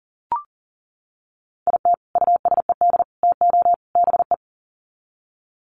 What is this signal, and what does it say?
Morse code, heard as a keyed tone, about 40 wpm, 700 Hz. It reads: IT USED TO BE
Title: Sets of 4 Words 40wpm